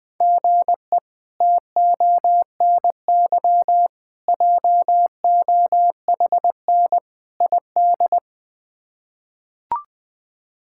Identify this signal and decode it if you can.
Morse code, heard as a keyed tone, about 20 wpm, 700 Hz. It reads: GE TONY JOHN ID
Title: K1USN SST QSO Practice - 20wpm